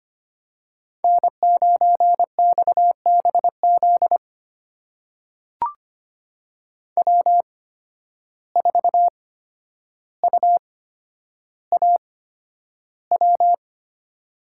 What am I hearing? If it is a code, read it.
Morse code, heard as a keyed tone, about 25 wpm, spaced as for 11 wpm, 700 Hz. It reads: N9XBZ W4UAW
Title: USA Call Signs - 5 Characters - 8x Character Spacing - 25wpm